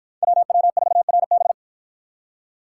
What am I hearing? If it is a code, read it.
Morse code, heard as a keyed tone, about 48 wpm, 700 Hz. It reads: WW4RB